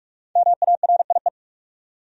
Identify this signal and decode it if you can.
Morse code, heard as a keyed tone, about 46 wpm, 700 Hz. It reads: MARIE
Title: USA Common Names - Max 5 Letters - 45wpm